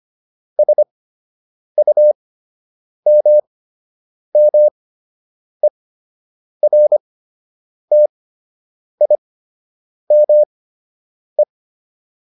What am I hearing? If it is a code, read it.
Morse code, heard as a keyed tone, about 25 wpm, spaced as for 8 wpm, 600 Hz. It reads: SUMMERTIME